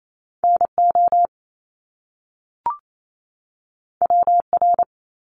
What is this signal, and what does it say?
Morse code, heard as a keyed tone, about 28 wpm, 700 Hz. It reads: NO WR
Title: Top Two Letter Combinations in Words 28wpm